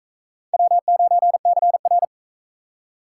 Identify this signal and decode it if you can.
Morse code, heard as a keyed tone, about 40 wpm, 700 Hz. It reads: W9CR